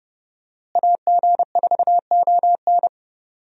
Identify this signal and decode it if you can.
Morse code, heard as a keyed tone, about 30 wpm, 700 Hz. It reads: AG4ON
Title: USA Call Signs - 5 Characters - Repeated 2x - 30wpm